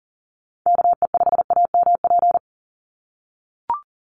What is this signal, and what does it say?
Morse code, heard as a keyed tone, about 40 wpm, 700 Hz. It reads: KE5AMP
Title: USA Call Signs 40wpm